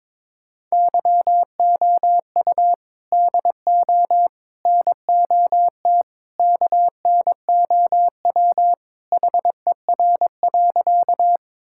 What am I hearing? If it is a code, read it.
Morse code, heard as a keyed tone, about 22 wpm, 700 Hz. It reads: YOU DO NOT KNOW HER.
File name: practice / 22wpm / Sentences with 4-Word Phrases - 22wpm